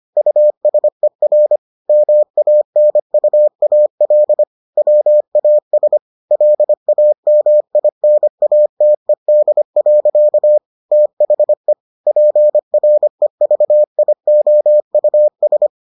Morse code, heard as a keyed tone, about 25 wpm, 600 Hz. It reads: USER MANUAL WAS LAMINATED. THE PREVIOUS